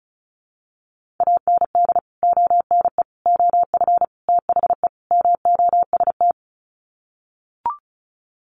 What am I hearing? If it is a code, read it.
Morse code, heard as a keyed tone, about 34 wpm, 700 Hz. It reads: AND ONE OF THE MOST
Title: Sets of 5 Words Encore 35wpm